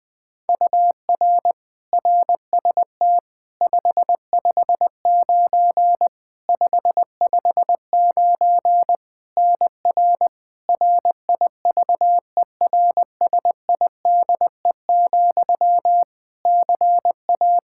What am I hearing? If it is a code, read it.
Morse code, heard as a keyed tone, about 20 wpm, 700 Hz. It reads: UR RST 559 559 NR RIVERSIDE, CA
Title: Practice QSO - Exchange 1 - Beginning - 20wpm